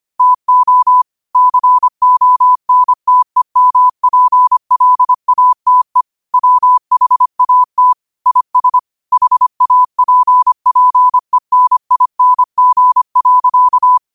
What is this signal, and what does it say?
Morse code, heard as a keyed tone, about 25 wpm, 1000 Hz. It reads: TO CONTEMPLATE WHAT IS HAPPENING.
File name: qotdfiles / 25wpm